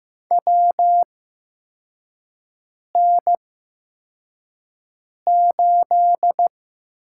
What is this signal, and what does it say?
Morse code, heard as a keyed tone, about 15 wpm, spaced as for 4 wpm, 700 Hz. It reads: WN8
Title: USA Call Signs - 6 Characters - 8x Character Spacing - 15wpm